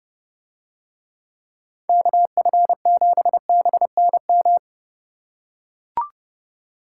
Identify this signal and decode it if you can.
Morse code, heard as a keyed tone, about 29 wpm, 700 Hz. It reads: KF7BNM